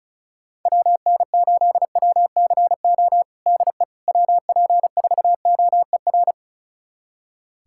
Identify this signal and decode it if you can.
Morse code, heard as a keyed tone, about 34 wpm, 690 Hz. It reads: WN8WCO DE WP4OER
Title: USA Callsign DE Callsign - 6 Characters - 35wpm